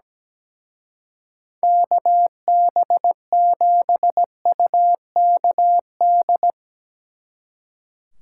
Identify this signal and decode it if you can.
Morse code, heard as a keyed tone, about 18 wpm, 700 Hz. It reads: KB7UKD
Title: Speed Racing - USA 2x3 Call Signs 17wpm